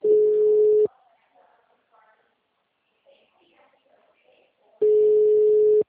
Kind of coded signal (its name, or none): telephone call progress tone